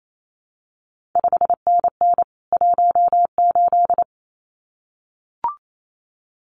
Morse code, heard as a keyed tone, about 28 wpm, 700 Hz. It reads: 5NN 18
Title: CQ WW DX Contest Exchange Encore - 28wpm